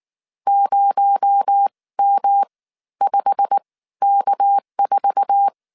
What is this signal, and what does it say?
Morse code, heard as a keyed tone, about 19 wpm, spaced as for 14 wpm, 800 Hz. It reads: 0M5K4
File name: CW / DET / 1 9 1500C S A 18wpm 2459